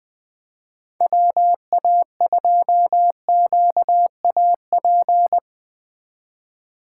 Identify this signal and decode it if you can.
Morse code, heard as a keyed tone, about 21 wpm, 700 Hz. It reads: WA2QAP
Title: USA Call Signs - 6 Characters - 3x Character Spacing - 20wpm